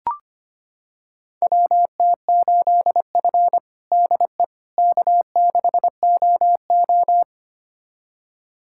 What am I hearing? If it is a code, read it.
Morse code, heard as a keyed tone, about 25 wpm, 700 Hz. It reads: WT8F DE K6OO
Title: USA Callsign DE Callsign - 4 Characters - 25wpm